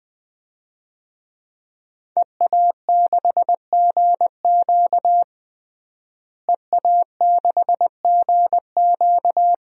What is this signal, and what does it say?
Morse code, heard as a keyed tone, about 20 wpm, 700 Hz. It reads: EA6GQ EA6GQ
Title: DXCC-100 Call Signs - 5 Characters - Repeated 2x - 20wpm